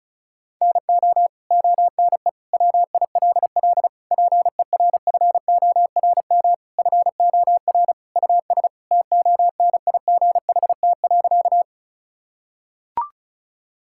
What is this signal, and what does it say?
Morse code, heard as a keyed tone, about 34 wpm, 700 Hz. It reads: NO ONE WILL PERFORM FOR US TONIGHT.
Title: Sentences from Top 1000 Words - Review Part 5 - 35wpm